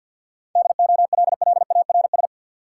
Morse code, heard as a keyed tone, about 47 wpm, 700 Hz. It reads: DOLLARS
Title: Sets of 5 Words - New Word Drill - 50wpm